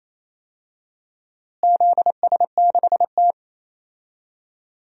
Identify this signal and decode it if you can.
Morse code, heard as a keyed tone, about 28 wpm, 700 Hz. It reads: ZS6T
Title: DXCC-100 Call Signs - 4 Characters - Repeated 3x - 28wpm